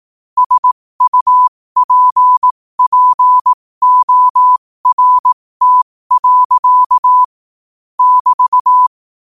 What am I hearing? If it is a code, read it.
Morse code, heard as a keyed tone, about 18 wpm, spaced as for 15 wpm, 1000 Hz. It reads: SUPPORT. <BT>
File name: qotdfiles / 15wpm